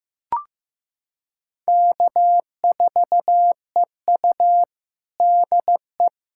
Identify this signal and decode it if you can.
Morse code, heard as a keyed tone, about 15 wpm, 700 Hz. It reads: K4EU DE
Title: Practice QSO - Exchange 4 - Ending - 15wpm